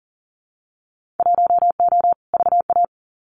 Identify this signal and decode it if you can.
Morse code, heard as a keyed tone, about 40 wpm, 700 Hz. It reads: 1O VA
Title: Winter Field Day Exchanges 40wpm